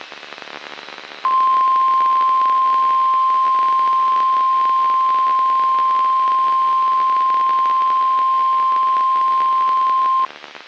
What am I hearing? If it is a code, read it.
Emergency Alert System (SAME) tones (the 1050 Hz attention tone). It is heard from 1.2 to 10.3 s.
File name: eas messages / EAS WXM95 Towanda